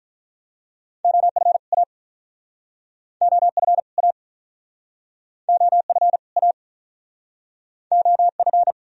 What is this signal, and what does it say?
Morse code, heard as a keyed tone, about 51 wpm, 690 Hz. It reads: OF A OF A OF A OF